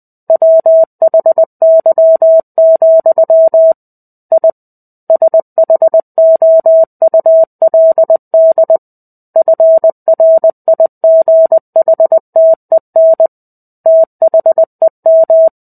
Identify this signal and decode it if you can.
Morse code, heard as a keyed tone, about 20 wpm, 650 Hz. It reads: WHY, I SHOULD FRIGHTEN THEM
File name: 05-Chapter-05 20 WPM